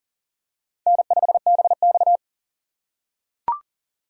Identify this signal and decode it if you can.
Morse code, heard as a keyed tone, about 40 wpm, 690 Hz. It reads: N5BX
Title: USA Call Signs - 8x Character Spacing - 40wpm